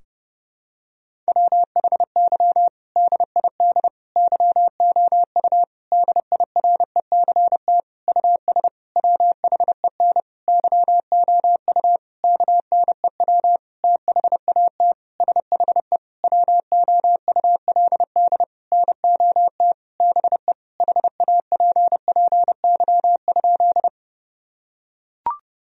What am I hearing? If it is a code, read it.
Morse code, heard as a keyed tone, about 30 wpm, 700 Hz. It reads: WHY DID YOU DIRECT US WHEN YOU KNEW THAT SHE WOULD NOT BE HAPPY?
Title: Sentences from Top 1000 Words - Review Part 2 - 30wpm